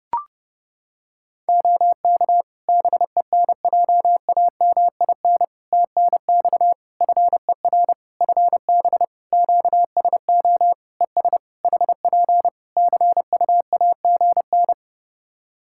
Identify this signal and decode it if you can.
Morse code, heard as a keyed tone, about 30 wpm, 700 Hz. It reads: OK BENJAMIN TNX FER FB QSO ES HP CUAGN
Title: K1USN SST QSO Practice 30wpm